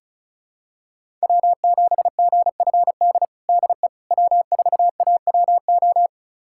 Morse code, heard as a keyed tone, about 36 wpm, 700 Hz. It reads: W7GFD DE W4AWO
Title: USA Callsign DE Callsign - 5 Characters - 35wpm